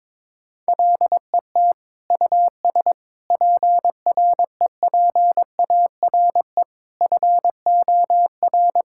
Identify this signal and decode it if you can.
Morse code, heard as a keyed tone, about 21 wpm, 700 Hz. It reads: LET US PREPARE FOR
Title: Sentences from Top 1000 Words - Review Part 6 - 22wpm